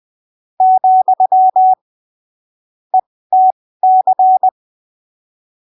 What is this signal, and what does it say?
Morse code, heard as a keyed tone, about 20 wpm, spaced as for 15 wpm, 750 Hz. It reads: , ETC
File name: arquivos / texto / texto 18 20ppm